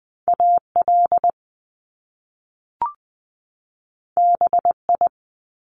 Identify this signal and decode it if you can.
Morse code, heard as a keyed tone, about 20 wpm, 700 Hz. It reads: AL BI
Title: Top Two Letter Combinations in Words 20wpm